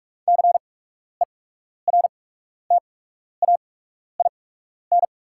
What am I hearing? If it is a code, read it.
Morse code, heard as a keyed tone, about 48 wpm, spaced as for 12 wpm, 690 Hz. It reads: CERTAIN
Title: 8x Character Spacing - Top 500 Words - 45wpm